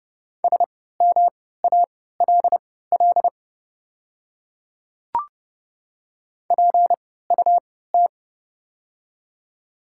Morse code, heard as a keyed tone, about 31 wpm, spaced as for 17 wpm, 700 Hz. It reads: SMALL PUT
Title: 3x Character Spacing - Top 200 Words - 30wpm